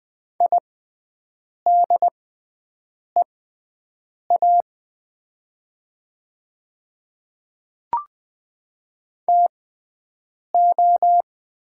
Morse code, heard as a keyed tone, about 20 wpm, spaced as for 7 wpm, 700 Hz. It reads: IDEA TO
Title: 6x Character Spacing - Top 300 Words - 20wpm